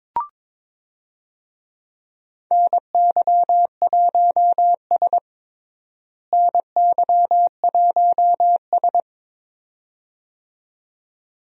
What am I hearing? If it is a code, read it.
Morse code, heard as a keyed tone, about 22 wpm, 700 Hz. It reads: NY1S NY1S
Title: USA Call Signs - 4 Characters - Repeated 2x - 22wpm